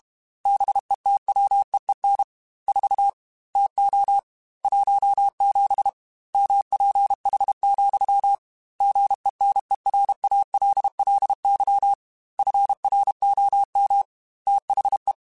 Morse code, heard as a keyed tone, about 32 wpm, 800 Hz. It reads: BETWEEN 4 TO 17 MPH, GENERALLY FROM THE
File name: Weather Sets / Weather Sets-mp3s / Weather 4 21 @30